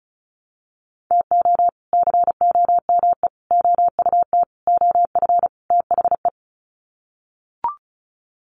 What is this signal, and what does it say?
Morse code, heard as a keyed tone, about 34 wpm, 700 Hz. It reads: TO COME OUT OF THE
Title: Sets of 5 Words Encore2 35wpm